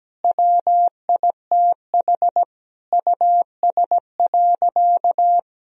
Morse code, heard as a keyed tone, about 17 wpm, 700 Hz. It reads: WITH US.
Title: Sentences from Top 1000 Words - Review Part 2 - 17wpm